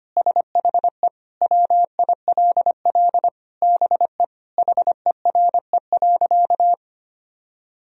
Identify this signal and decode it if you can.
Morse code, heard as a keyed tone, about 25 wpm, 700 Hz. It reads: SHE WILL BE HERE.
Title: Sentences from Top 1000 Words - Review Part 4 - 25wpm